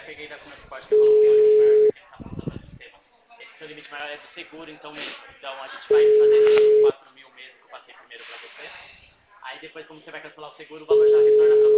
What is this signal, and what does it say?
Ringback tone (Europe, 425 Hz), with speech.